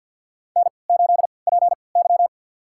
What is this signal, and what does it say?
Morse code, heard as a keyed tone, about 48 wpm, spaced as for 28 wpm, 700 Hz. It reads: N8PY
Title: USA Call Signs - 4 Characters - 3x Character Spacing - 50wpm